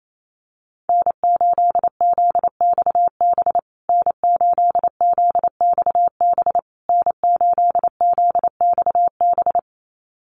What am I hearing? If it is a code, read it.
Morse code, heard as a keyed tone, about 28 wpm, 700 Hz. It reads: N8ZXB N8ZXB N8ZXB
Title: USA Call Signs Repeated 3x 28wpm